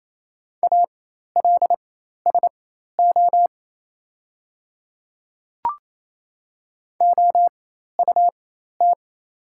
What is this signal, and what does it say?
Morse code, heard as a keyed tone, about 28 wpm, spaced as for 13 wpm, 700 Hz. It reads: ALSO OUT
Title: 4x Character Spacing - Top 200 Words - 28wpm